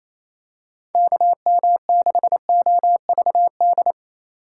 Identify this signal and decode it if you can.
Morse code, heard as a keyed tone, about 28 wpm, 700 Hz. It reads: KM6OVD
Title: USA Call Signs - 4x Character Spacing - 28wpm